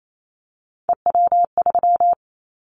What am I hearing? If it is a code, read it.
Morse code, heard as a keyed tone, about 28 wpm, 700 Hz. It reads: EW3
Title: Callsign Prefixes for Top 100 DXCC - 28wpm